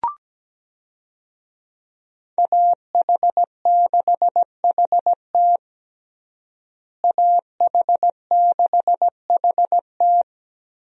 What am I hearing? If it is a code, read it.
Morse code, heard as a keyed tone, about 17 wpm, 700 Hz. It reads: AH6HT AH6HT
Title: USA Call Signs - 5 Characters - Repeated 3x - 17wpm